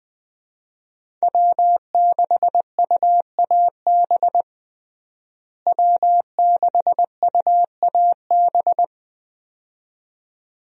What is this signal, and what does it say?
Morse code, heard as a keyed tone, about 20 wpm, 700 Hz. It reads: W6UAB W6UAB
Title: USA Call Signs - 5 Characters - Repeated 2x - 20wpm